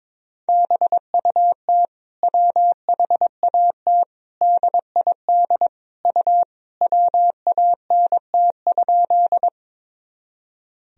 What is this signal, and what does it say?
Morse code, heard as a keyed tone, about 21 wpm, 700 Hz. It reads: BUT WHAT DID U WANT?
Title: Sentences from Top 200 Words and Abbreviations 22wpm